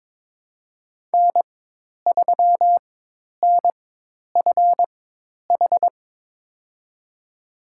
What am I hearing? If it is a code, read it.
Morse code, heard as a keyed tone, about 22 wpm, spaced as for 10 wpm, 700 Hz. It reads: N3NFH
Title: USA Call Signs - 5 Characters - 4x Character Spacing - 22wpm